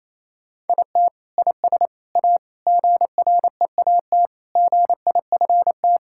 Morse code, heard as a keyed tone, about 28 wpm, 700 Hz. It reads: IT IS A GREAT GIFT
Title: Sentences from Top 1000 Words - Review Part 6 - 28wpm